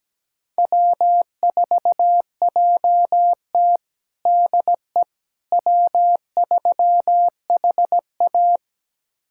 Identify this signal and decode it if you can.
Morse code, heard as a keyed tone, about 17 wpm, 700 Hz. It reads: W4JT DE W3HA
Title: USA Callsign DE Callsign - 4 Characters - 17wpm